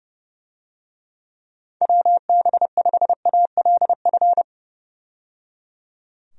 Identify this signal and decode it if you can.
Morse code, heard as a keyed tone, about 30 wpm, 700 Hz. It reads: WB5ALF
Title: USA Call Signs - 6 Characters - 30wpm